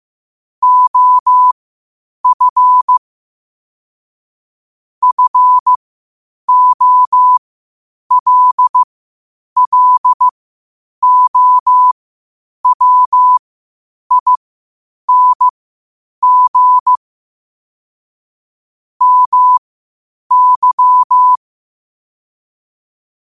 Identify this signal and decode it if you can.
Morse code, heard as a keyed tone, about 15 wpm, spaced as for 8 wpm, 1000 Hz. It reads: OF FOLLOWING MY
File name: qotdfiles / 8wpm